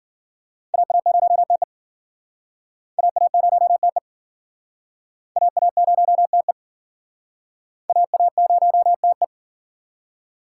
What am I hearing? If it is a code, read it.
Morse code, heard as a keyed tone, about 52 wpm, 700 Hz. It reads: AA0TE AA0TE AA0TE AA0TE